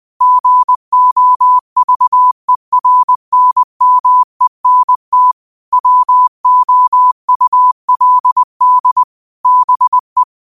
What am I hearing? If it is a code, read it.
Morse code, heard as a keyed tone, about 20 wpm, 1000 Hz. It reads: GOVERNMENT WOULD BE